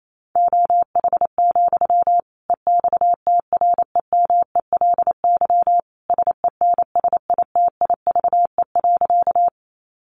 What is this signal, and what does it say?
Morse code, heard as a keyed tone, about 28 wpm, 700 Hz. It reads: OH, EXTREMELY SENSITIVE.
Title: Sentences from Top 1400 Words 28wpm